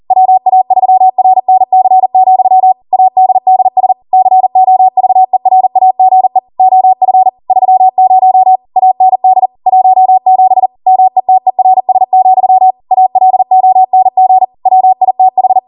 Morse code, heard as a keyed tone, about 40 wpm, 750 Hz. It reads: WA3RNC, ADDS COVERAGE OF 30 AND 17 METERS, ALONG WITH
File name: Morse / 40WPM